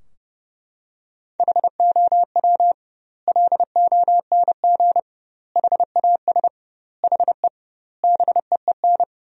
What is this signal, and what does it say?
Morse code, heard as a keyed tone, about 32 wpm, 700 Hz. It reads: HOW LONG HAS HE BEEN
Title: 2x Word-Spacing - Sentences from Top 100 Words - 30wpm